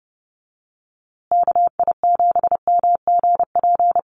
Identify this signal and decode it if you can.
Morse code, heard as a keyed tone, about 30 wpm, 700 Hz. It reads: KI7MGP
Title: USA Call Signs Repeated 2x 30wpm